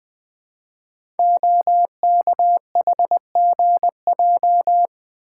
Morse code, heard as a keyed tone, about 20 wpm, 700 Hz. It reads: OKHGJ
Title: Code Groups - 5 Letter:Numbers - 20wpm